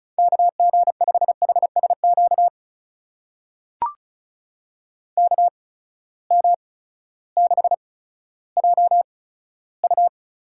Morse code, heard as a keyed tone, about 36 wpm, spaced as for 16 wpm, 700 Hz. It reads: KG5HSQ KM6JU
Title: USA Call Signs - 6 Characters - 8x Character Spacing - 35wpm